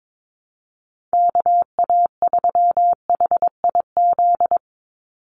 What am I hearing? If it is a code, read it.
Morse code, heard as a keyed tone, about 22 wpm, 700 Hz. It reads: KA3HIZ
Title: USA Call Signs 22wpm